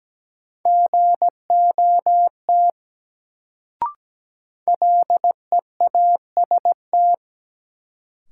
Morse code, heard as a keyed tone, about 17 wpm, 700 Hz. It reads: GOT LEAST
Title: Sets of 2 Words - New Word Drill - 17wpm